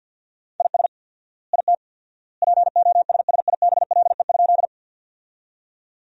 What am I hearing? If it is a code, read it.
Morse code, heard as a keyed tone, about 51 wpm, 700 Hz. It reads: IS IT POSSIBLE?
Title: 4x Word-Spacing - Sentences from Top 500 Words - 50wpm